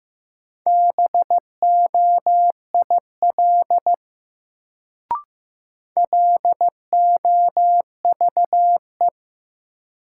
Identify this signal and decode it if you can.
Morse code, heard as a keyed tone, about 15 wpm, 700 Hz. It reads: BOIL LOVE
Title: Sets of 5 Words Encore - New Word Drill - 15wpm